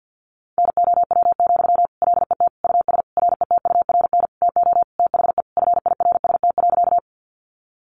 Morse code, heard as a keyed tone, about 50 wpm, 700 Hz. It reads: NOW, LET US RETURN TO THE FIRST.
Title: Sentences from Top 800 Words 50wpm